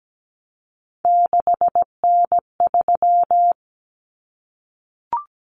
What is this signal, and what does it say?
Morse code, heard as a keyed tone, about 17 wpm, 700 Hz. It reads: 6N3
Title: Common Cut Numbers - 3-Digit 17wpm